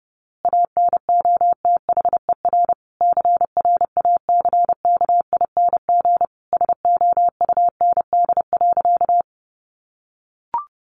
Morse code, heard as a keyed tone, about 28 wpm, 700 Hz. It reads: ANOTHER CRACKING SOUND.
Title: A Sound Of Thunder 30wpm